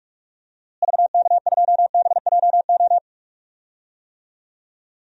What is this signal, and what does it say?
Morse code, heard as a keyed tone, about 44 wpm, 700 Hz. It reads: VK2BJO
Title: DXCC-100 Call Signs - 6 Characters - Repeated 2x - 45wpm